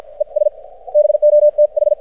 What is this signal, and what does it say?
Morse code, heard as a keyed tone, about 54 wpm, spaced as for 45 wpm, 600 Hz. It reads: ES BOTH
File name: archived / CRSnet